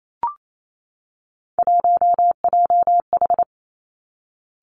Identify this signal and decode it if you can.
Morse code, heard as a keyed tone, about 28 wpm, 700 Hz. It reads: 1JH
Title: Three Challenging Letter-Number Combinations 28wpm